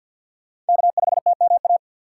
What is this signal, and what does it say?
Morse code, heard as a keyed tone, about 50 wpm, 700 Hz. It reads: K5TMA
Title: USA Call Signs - 2x Character Spacing - 50wpm